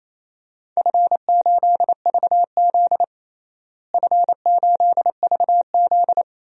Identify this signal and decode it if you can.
Morse code, heard as a keyed tone, about 27 wpm, 700 Hz. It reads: F8VZ F8VZ
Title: DXCC-100 Call Signs - 4 Characters - Repeated 3x - 28wpm